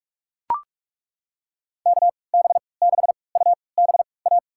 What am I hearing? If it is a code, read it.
Morse code, heard as a keyed tone, about 41 wpm, spaced as for 25 wpm, 700 Hz. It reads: KB6UBA
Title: USA Call Signs - 3x Character Spacing - 45wpm